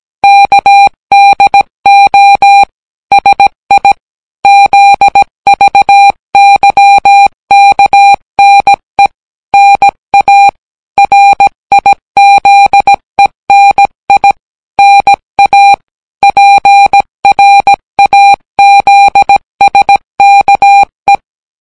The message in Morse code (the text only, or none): KDO SI ZVYKNE NA RIZENI NA PRAZSKE